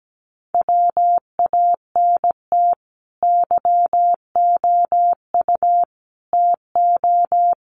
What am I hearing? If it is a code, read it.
Morse code, heard as a keyed tone, about 17 wpm, 700 Hz. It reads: WANT YOU TO
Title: Sets of 5 Words Encore 17wpm